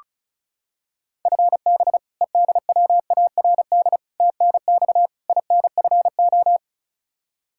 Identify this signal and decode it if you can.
Morse code, heard as a keyed tone, about 34 wpm, 700 Hz. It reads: FB EDWARD TNX INFO
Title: Practice QSO - Exchange 3 - More Info - 35wpm